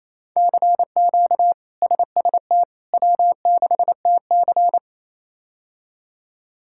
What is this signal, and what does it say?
Morse code, heard as a keyed tone, about 27 wpm, 700 Hz. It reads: CQ SST W6TC
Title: K1USN SST QSO Practice - 28wpm